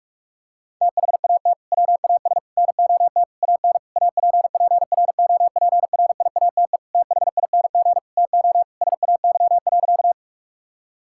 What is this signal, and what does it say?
Morse code, heard as a keyed tone, about 42 wpm, 700 Hz. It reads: THAT WAS NOT AN APPROPRIATE THING TO SAY.
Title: Sentences from Top 1000 Words - Review Part 5 - 45wpm